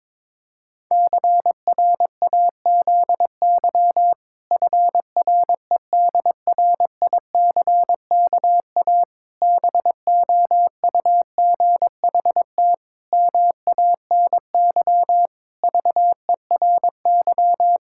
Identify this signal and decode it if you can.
Morse code, heard as a keyed tone, about 22 wpm, 700 Hz. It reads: CRAZY FREDRICKA BOUGHT MANY VERY
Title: Pangram Challenge v2 - 22wpm